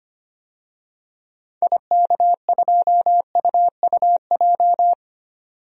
Morse code, heard as a keyed tone, about 25 wpm, 700 Hz. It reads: IK2UUJ
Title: DXCC-100 Call Signs - 6 Characters - 25wpm